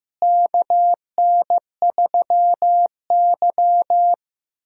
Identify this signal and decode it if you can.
Morse code, heard as a keyed tone, about 15 wpm, 700 Hz. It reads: KN3Y
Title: USA Call Signs - 4 Characters - Repeated 3x - 15wpm